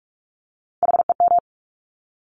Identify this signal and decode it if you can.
Morse code, heard as a keyed tone, about 46 wpm, 700 Hz. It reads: HEM
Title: Top Three Letter Combinations in Words 45wpm